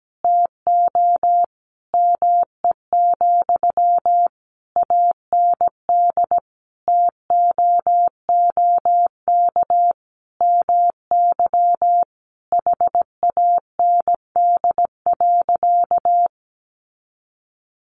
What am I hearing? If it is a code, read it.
Morse code, heard as a keyed tone, about 17 wpm, 700 Hz. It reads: TO ME, AND TOOK MY HAND.
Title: Sentences from Top 1100 Words 17wpm